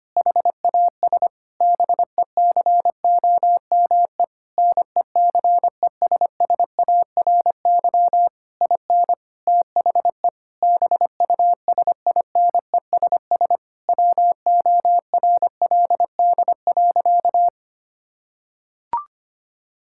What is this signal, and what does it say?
Morse code, heard as a keyed tone, about 25 wpm, 700 Hz. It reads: HAS BECOME NECESSARY IN THE BUSINESS WORLD.
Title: Sentences from Top 1000 Words - Review Part 5 - 25wpm